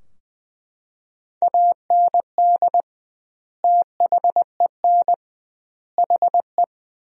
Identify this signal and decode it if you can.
Morse code, heard as a keyed tone, about 20 wpm, 700 Hz. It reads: AND THEN HE
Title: 2x Word-Spacing - Sentences from Top 400 Words - 20wpm